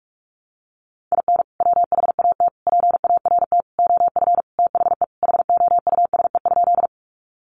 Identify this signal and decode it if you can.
Morse code, heard as a keyed tone, about 44 wpm, 700 Hz. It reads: IN WHAT PART OF THE HOUSE?
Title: Sentences from Top 200 Words 45wpm